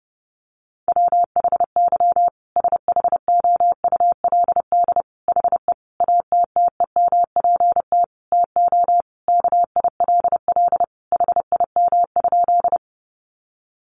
Morse code, heard as a keyed tone, about 30 wpm, 700 Hz. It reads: WHY SHOULD HE ATTEMPT TO KILL HIM?
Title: Sentences from Top 1400 Words 30wpm